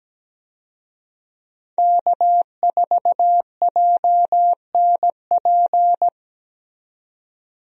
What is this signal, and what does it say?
Morse code, heard as a keyed tone, about 17 wpm, 700 Hz. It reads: K4JNP
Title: USA Call Signs - 5 Characters - 17wpm